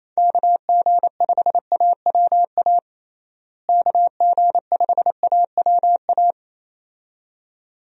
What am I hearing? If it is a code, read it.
Morse code, heard as a keyed tone, about 28 wpm, 700 Hz. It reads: KG5AWA KG5AWA
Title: USA Call Signs - 6 Characters - Repeated 2x - 28wpm